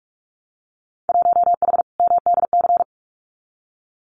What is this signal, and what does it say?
Morse code, heard as a keyed tone, about 46 wpm, 700 Hz. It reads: 1H MDC
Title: Winter Field Day Exchanges 45wpm